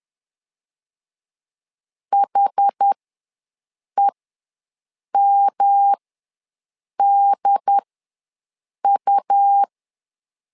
Morse code, heard as a keyed tone, about 10 wpm, spaced as for 6 wpm, 800 Hz. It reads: HEMDU
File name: LEZIONI / 3 MWRO / 3 M ws 10wpm 2121